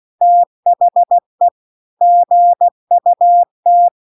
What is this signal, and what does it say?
Morse code, heard as a keyed tone, about 16 wpm, 700 Hz. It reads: THE GUT